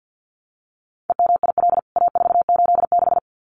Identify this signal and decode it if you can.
Morse code, heard as a keyed tone, about 49 wpm, 700 Hz. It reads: EMIL A486
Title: CWT Exchange 50wpm